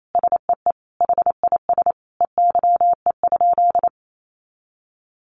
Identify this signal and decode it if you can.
Morse code, heard as a keyed tone, about 28 wpm, 700 Hz. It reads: SEE HIS EYE?
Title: Sentences from Top 300 Words 28wpm